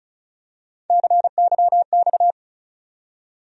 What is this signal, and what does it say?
Morse code, heard as a keyed tone, about 36 wpm, 700 Hz. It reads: CYX
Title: USA Call Sign Postfixes - 3 Letters 35wpm